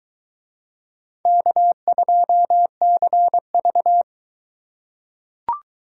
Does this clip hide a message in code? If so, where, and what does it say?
Yes: Morse code from 1.3 to 4.0 s, heard as a keyed tone, about 21 wpm, 700 Hz. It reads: K2CV